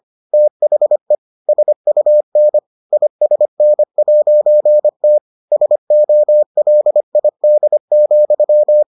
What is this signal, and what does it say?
Morse code, heard as a keyed tone, about 25 wpm, 600 Hz. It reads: THE SUN ISN'T SOLID,